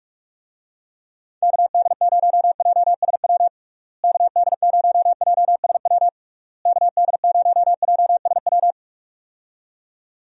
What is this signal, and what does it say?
Morse code, heard as a keyed tone, about 42 wpm, 700 Hz. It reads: KD0JSW KD0JSW KD0JSW
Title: USA Call Signs - 6 Characters - Repeated 3x - 45wpm